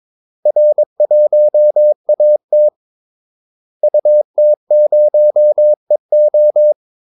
Morse code, heard as a keyed tone, about 22 wpm, 600 Hz. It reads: R1AT UT0EO
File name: Callsigns-22wpm